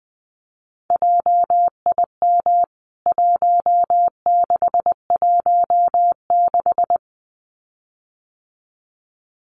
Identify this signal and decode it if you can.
Morse code, heard as a keyed tone, about 20 wpm, 700 Hz. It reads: JIM 1616
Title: CWT Exchange 20wpm